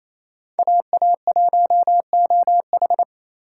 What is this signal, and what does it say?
Morse code, heard as a keyed tone, about 28 wpm, 700 Hz. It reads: AA1OH
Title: USA Call Signs - 4x Character Spacing - 28wpm